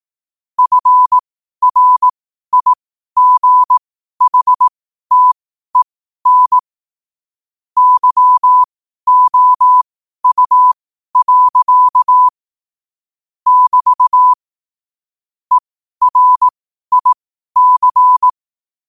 Morse code, heard as a keyed tone, about 18 wpm, spaced as for 12 wpm, 1000 Hz. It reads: FRIGHTEN YOU. <BT> ERIC
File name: qotdfiles / 12wpm